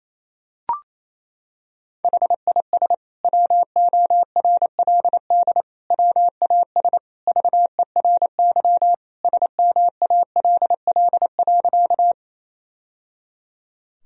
Morse code, heard as a keyed tone, about 28 wpm, 700 Hz. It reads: HIS WORLD WAS VERY SMALL.
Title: Sentences from Top 1000 Words - Review Part 1 - 28wpm